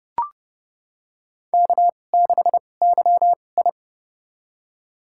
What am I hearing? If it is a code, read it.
Morse code, heard as a keyed tone, about 30 wpm, spaced as for 22 wpm, 700 Hz. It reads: K6YI
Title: USA Call Signs - 2x Character Spacing - 30wpm